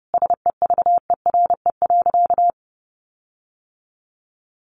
Morse code, heard as a keyed tone, about 30 wpm, 700 Hz. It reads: SEVERE.
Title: Sentences from Top 1400 Words 30wpm